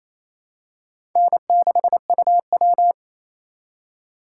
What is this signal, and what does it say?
Morse code, heard as a keyed tone, about 28 wpm, 700 Hz. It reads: N6UW
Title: USA Call Signs - 4 Characters - 2x Character Spacing - 28wpm